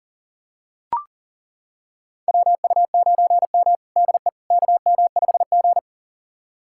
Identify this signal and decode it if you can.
Morse code, heard as a keyed tone, about 37 wpm, 700 Hz. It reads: WU9M DE KM5G
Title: USA Callsign DE Callsign - 4 Characters - 40wpm